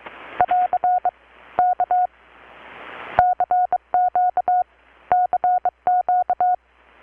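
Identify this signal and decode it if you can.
Morse code, heard as a keyed tone, about 23 wpm, 720 Hz. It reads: <AR> K CQ CQ
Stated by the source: Morse from KPH's Night of Nights XX maritime radiotelegraphy event, received in Chiapas, Mexico about 3,800 km away on a Kenwood R-600 with a grounded 10 m long-wire antenna.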